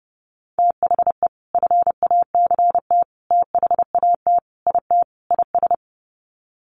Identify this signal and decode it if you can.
Morse code, heard as a keyed tone, about 30 wpm, 700 Hz. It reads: THE FACT THAT IT IS
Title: Sets of 5 Words Encore2 30wpm